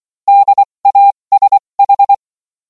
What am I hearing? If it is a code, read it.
Morse code, heard as a keyed tone, about 24 wpm, spaced as for 20 wpm, 800 Hz. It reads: DASH